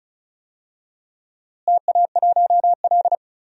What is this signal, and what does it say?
Morse code, heard as a keyed tone, about 36 wpm, 700 Hz. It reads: TA1L
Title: DXCC-100 Call Signs - 4 Characters - Repeated 2x - 35wpm